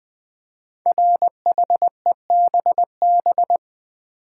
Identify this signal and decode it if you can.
Morse code, heard as a keyed tone, about 20 wpm, 700 Hz. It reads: RHEBB
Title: Code Groups - 5 Letters - 20wpm